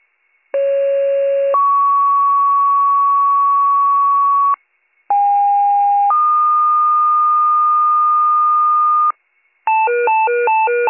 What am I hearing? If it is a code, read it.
Two-tone page, heard as a one-second tone, then a three-second tone, about 550 then 1120 Hz, about 810 then 1220 Hz. It is heard from 0.5 to 9.1 s.